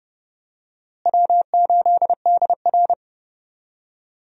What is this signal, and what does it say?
Morse code, heard as a keyed tone, about 30 wpm, 700 Hz. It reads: W8DR
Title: USA Call Signs - 4 Characters - Repeated 2x - 30wpm